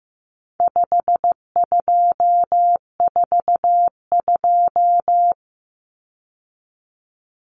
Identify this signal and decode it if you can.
Morse code, heard as a keyed tone, about 15 wpm, 700 Hz. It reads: 5242
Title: Numbers - 4-Digit 15wpm